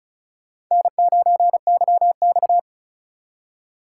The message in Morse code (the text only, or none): N9YX